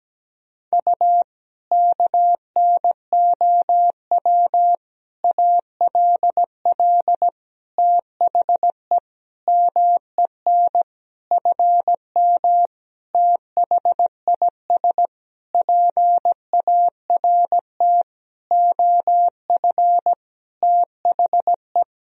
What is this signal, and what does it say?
Morse code, heard as a keyed tone, about 17 wpm, 700 Hz. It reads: U KNOW ALL THE MEN FM THIS PART OF THE
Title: Sentences from Top 300 Words and Abbreviations 17wpm